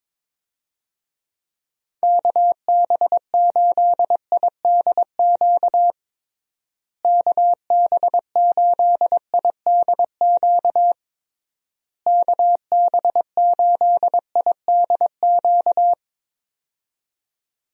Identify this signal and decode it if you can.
Morse code, heard as a keyed tone, about 22 wpm, 700 Hz. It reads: KB8IDQ KB8IDQ KB8IDQ
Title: USA Call Signs - 6 Characters - Repeated 3x - 22wpm